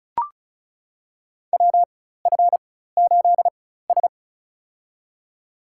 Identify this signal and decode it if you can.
Morse code, heard as a keyed tone, about 36 wpm, spaced as for 16 wpm, 700 Hz. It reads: WF8S
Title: USA Call Signs - 4 Characters - 4x Character Spacing - 35wpm